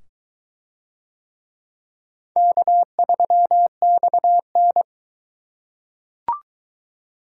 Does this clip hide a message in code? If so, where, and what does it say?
Yes: Morse code from 2.4 to 4.8 s, heard as a keyed tone, about 21 wpm, 700 Hz. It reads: K3XN